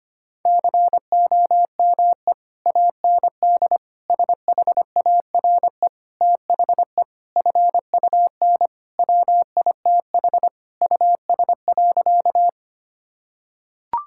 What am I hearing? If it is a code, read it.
Morse code, heard as a keyed tone, about 25 wpm, 700 Hz. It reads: COME AND SHARE THE FUN WITH US.
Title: Sentences from Top 1000 Words - Review Part 6 - 25wpm